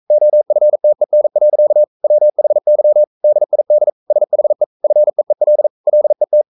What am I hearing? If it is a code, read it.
Morse code, heard as a keyed tone, about 42 wpm, 600 Hz. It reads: OFTEN. WHY DID SHE FEEL LET
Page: Romanen Never af Ken Follet på engelsk.